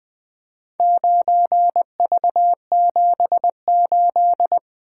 Morse code, heard as a keyed tone, about 20 wpm, 700 Hz. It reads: 9V78
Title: Four Letter-Number Combinations - 20wpm